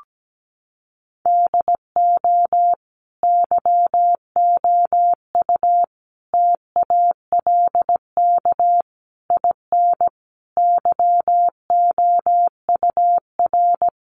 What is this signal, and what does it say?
Morse code, heard as a keyed tone, about 17 wpm, 700 Hz. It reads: DO YOU TALK IN YOUR
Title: Sentences from Top 500 Words 17wpm